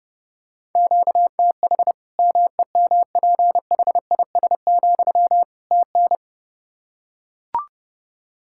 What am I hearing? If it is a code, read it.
Morse code, heard as a keyed tone, about 30 wpm, 700 Hz. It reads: QTH MEMPHIS, TN
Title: Practice QSO - Exchange 1 - Beginning - 30wpm